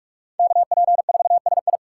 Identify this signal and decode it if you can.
Morse code, heard as a keyed tone, about 46 wpm, 700 Hz. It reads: KP4SI
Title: USA Call Signs - 5 Characters - 3x Character Spacing - 45wpm